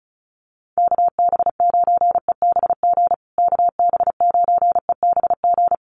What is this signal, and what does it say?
Morse code, heard as a keyed tone, about 34 wpm, 700 Hz. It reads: KB9EBG KB9EBG
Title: USA Call Signs Repeated 2x 35wpm